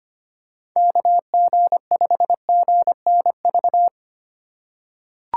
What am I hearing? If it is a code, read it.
Morse code, heard as a keyed tone, about 25 wpm, 700 Hz. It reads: KG5GNV
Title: USA Call Signs - 6 Characters - 4x Character Spacing - 25wpm